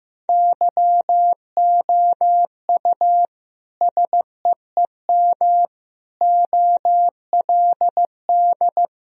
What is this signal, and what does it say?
Morse code, heard as a keyed tone, about 15 wpm, 700 Hz. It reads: YOU SEEM OLD
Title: Sentences from Top 1000 Words - Review Part 1 - 15wpm